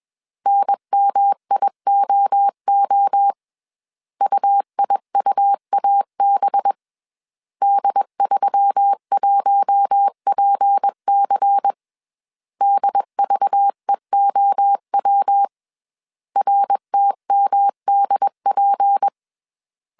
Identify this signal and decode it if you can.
Morse code, heard as a keyed tone, about 21 wpm, 800 Hz. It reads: NMIOO UIUAB D31PC DVEOW RTMDP
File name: CW / DET / 1 6 1500C S B 20wpm 2048